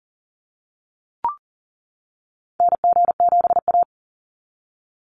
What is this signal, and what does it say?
Morse code, heard as a keyed tone, about 38 wpm, 700 Hz. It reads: NG7A
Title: NA SOTA Activator Callsigns 40wpm